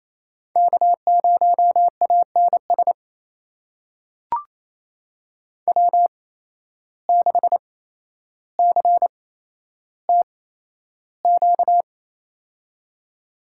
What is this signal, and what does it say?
Morse code, heard as a keyed tone, about 28 wpm, spaced as for 12 wpm, 700 Hz. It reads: K0ANS W6CTQ
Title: USA Call Signs - 5 Characters - 8x Character Spacing - 28wpm